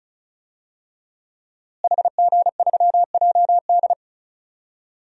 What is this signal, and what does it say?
Morse code, heard as a keyed tone, about 36 wpm, 700 Hz. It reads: HG3JD
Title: DXCC-100 Call Signs - 5 Characters - Repeated 3x - 35wpm